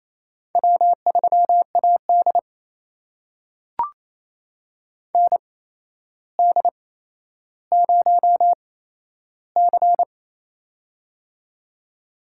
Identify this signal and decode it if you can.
Morse code, heard as a keyed tone, about 28 wpm, spaced as for 12 wpm, 700 Hz. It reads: W3AD ND0C
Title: USA Call Signs - 4 Characters - 8x Character Spacing - 28wpm